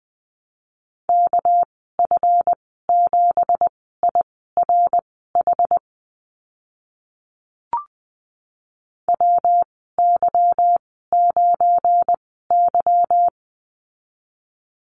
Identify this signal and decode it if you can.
Morse code, heard as a keyed tone, about 20 wpm, spaced as for 15 wpm, 700 Hz. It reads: KF7IRH WY9Y
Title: 200 Lesson 32g - Letter k - Call Signs 20wpm